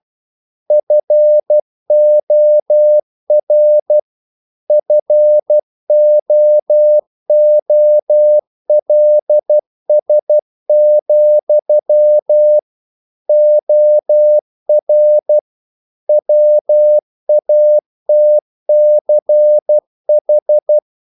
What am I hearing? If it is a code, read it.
Morse code, heard as a keyed tone, about 12 wpm, 600 Hz. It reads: FOR FOOLS, OR WATCH
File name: CW / Practice / If12